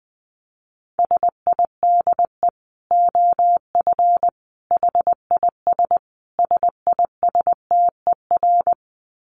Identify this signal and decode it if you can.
Morse code, heard as a keyed tone, about 20 wpm, 700 Hz. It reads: SIDE OF HIS SISTER
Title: Dit Challenge 20wpm